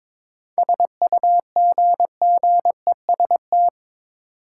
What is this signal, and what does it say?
Morse code, heard as a keyed tone, about 21 wpm, 700 Hz. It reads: SUGGEST
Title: Sentences from Top 1300 Words - New Word Drill - 22wpm